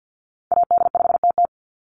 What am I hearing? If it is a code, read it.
Morse code, heard as a keyed tone, about 50 wpm, 700 Hz. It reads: AD5TT
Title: Call Signs - 50wpm